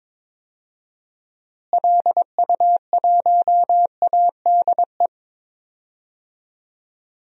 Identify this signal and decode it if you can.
Morse code, heard as a keyed tone, about 22 wpm, 700 Hz. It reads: LU1ADE
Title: DXCC-100 Call Signs - 6 Characters - Repeated 2x - 22wpm